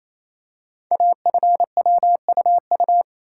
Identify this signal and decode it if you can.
Morse code, heard as a keyed tone, about 29 wpm, 690 Hz. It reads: AFWUU
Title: Code Groups - 5 Letters - 28wpm